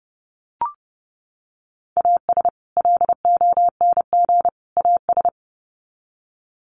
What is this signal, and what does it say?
Morse code, heard as a keyed tone, about 30 wpm, 700 Hz. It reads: AS LONG AS
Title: Sets of 3 Words 30wpm